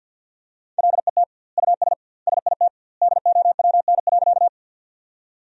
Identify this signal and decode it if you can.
Morse code, heard as a keyed tone, about 50 wpm, 700 Hz. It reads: LET US SIT DOWN.
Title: 2x Word-Spacing - Sentences from Top 500 Words - 50wpm